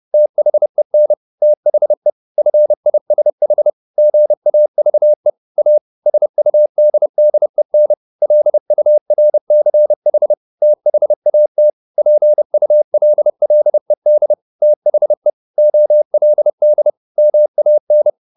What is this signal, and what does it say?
Morse code, heard as a keyed tone, about 30 wpm, 600 Hz. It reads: THEN THE FISH GAVE A SUDDEN LURCH THAT PULLED THE OLD MAN
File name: the old man and the sea / the old man and the sea 1 30wpm